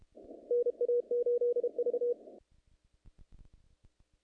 Morse code, heard as a keyed tone, about 31 wpm, 480 Hz. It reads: NA8V
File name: contests / naqp-cw-k5tr-op-k5pi- -2200